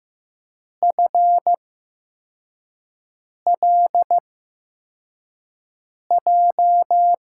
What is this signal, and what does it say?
Morse code, heard as a keyed tone, about 15 wpm, spaced as for 4 wpm, 700 Hz. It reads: FLJ
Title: USA Call Signs - 6 Characters - 8x Character Spacing - 15wpm